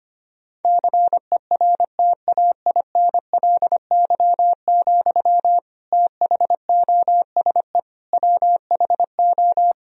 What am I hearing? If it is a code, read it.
Morse code, heard as a keyed tone, about 25 wpm, 700 Hz. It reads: CERTAINLY, THOSE WHO
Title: Sentences from Top 1000 Words - Review Part 2 - 25wpm